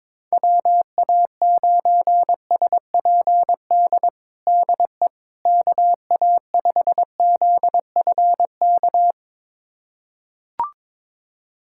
Morse code, heard as a keyed tone, about 22 wpm, 700 Hz. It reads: WA9SPD DE KA5ZFK
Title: USA Callsign DE Callsign - 6 Characters - 22wpm